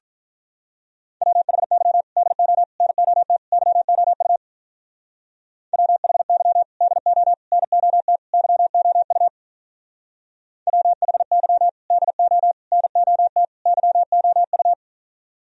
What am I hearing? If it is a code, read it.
Morse code, heard as a keyed tone, about 42 wpm, 700 Hz. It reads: WHY DO NOT YOU WHY DO NOT YOU WHY DO NOT YOU
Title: Speed-Racing - Sets of 4 Words 35wpm